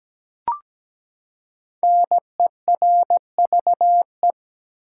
Morse code, heard as a keyed tone, about 17 wpm, 700 Hz. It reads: NERVE
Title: Five Letter Words - 17wpm